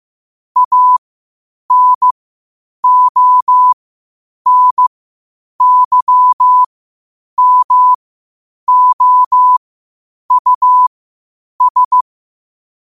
Morse code, heard as a keyed tone, about 15 wpm, spaced as for 8 wpm, 1000 Hz. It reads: ANONYMOUS
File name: qotdfiles / 8wpm